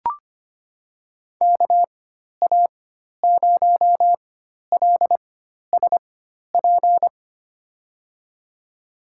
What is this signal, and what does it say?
Morse code, heard as a keyed tone, about 25 wpm, spaced as for 12 wpm, 700 Hz. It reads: KA0LSP
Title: USA Call Signs - 6 Characters - 4x Character Spacing - 25wpm